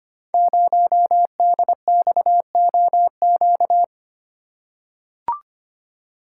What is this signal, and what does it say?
Morse code, heard as a keyed tone, about 25 wpm, 700 Hz. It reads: 0DXOQ
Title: Code Groups - 5 Letter:Numbers - 25wpm